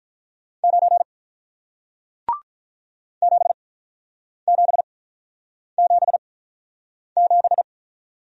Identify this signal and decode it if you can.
Morse code, heard as a keyed tone, about 47 wpm, spaced as for 9 wpm, 700 Hz. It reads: 9 7777